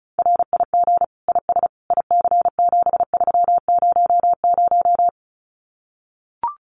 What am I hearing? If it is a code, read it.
Morse code, heard as a keyed tone, about 34 wpm, 700 Hz. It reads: RIG IS IC7300
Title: Common QSO Phrases Encore 35wpm